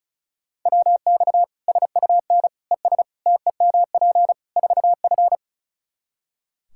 Morse code, heard as a keyed tone, about 34 wpm, 700 Hz. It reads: WX SUN ES TEMP 4F
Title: Practice QSO - Exchange 2 - Info - 35wpm